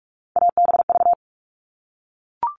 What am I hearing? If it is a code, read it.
Morse code, heard as a keyed tone, about 44 wpm, 700 Hz. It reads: ABV
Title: All Cut Numbers - 3-Digit 45wpm